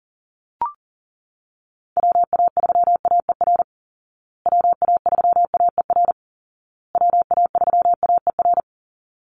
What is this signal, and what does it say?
Morse code, heard as a keyed tone, about 40 wpm, 700 Hz. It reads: WA3AER WA3AER WA3AER
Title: Call Signs Repeated 2x - 40wpm